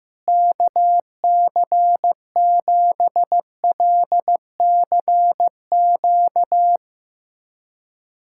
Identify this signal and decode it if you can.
Morse code, heard as a keyed tone, about 15 wpm, 700 Hz. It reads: KC7LCQ
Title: USA Callsign DE Callsign - 6 Characters - 15wpm